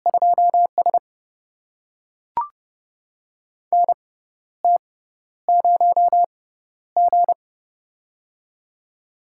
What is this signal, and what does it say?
Morse code, heard as a keyed tone, about 30 wpm, spaced as for 12 wpm, 700 Hz. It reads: 2S NT0G
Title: USA Call Signs - 6x Character Spacing - 30wpm